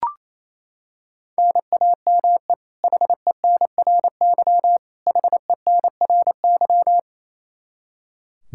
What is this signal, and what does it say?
Morse code, heard as a keyed tone, about 28 wpm, 700 Hz. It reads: NAME HENRY HENRY
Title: Practice QSO - Exchange 1 - Beginning - 28wpm